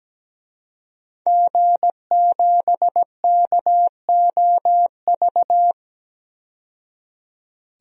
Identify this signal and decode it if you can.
Morse code, heard as a keyed tone, about 17 wpm, 700 Hz. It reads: G7KOV
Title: DXCC-100 Call Signs - 5 Characters - 17wpm